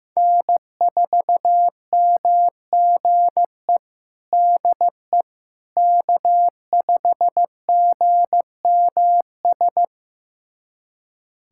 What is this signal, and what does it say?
Morse code, heard as a keyed tone, about 15 wpm, 700 Hz. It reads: N4MGE DE K5GMS
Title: USA Callsign DE Callsign - 5 Characters - 15wpm